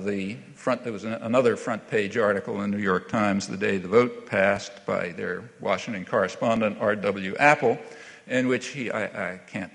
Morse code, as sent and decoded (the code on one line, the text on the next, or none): none